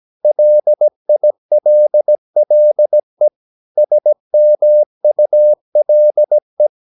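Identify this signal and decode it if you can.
Morse code, heard as a keyed tone, about 17 wpm, 600 Hz. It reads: LILLE SMULE
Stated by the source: Slow (QRS) Morse practice at 17 wpm, from CW med Gnister.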